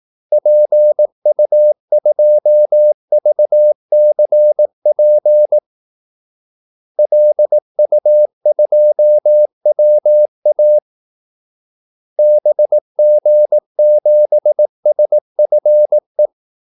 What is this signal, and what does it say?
Morse code, heard as a keyed tone, about 18 wpm, 600 Hz. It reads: PU2VCP LU2WA BG7SFE
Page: Kaldesignaler 18 wpm